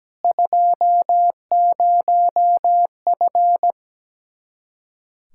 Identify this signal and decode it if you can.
Morse code, heard as a keyed tone, about 17 wpm, 700 Hz. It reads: 20F
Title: Speed-Racing Common QSO Phrases - 17wpm